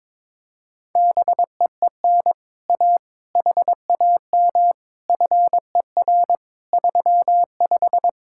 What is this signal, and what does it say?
Morse code, heard as a keyed tone, about 22 wpm, 700 Hz. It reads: BEEN A HAM FER 35
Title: Speed-Racing Common QSO Phrases Encore - 22wpm